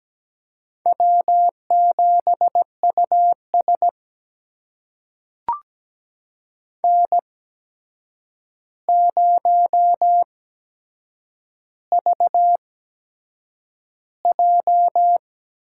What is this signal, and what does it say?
Morse code, heard as a keyed tone, about 17 wpm, spaced as for 7 wpm, 700 Hz. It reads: W7US N0VJ
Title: USA Call Signs - 4 Characters - 8x Character Spacing - 17wpm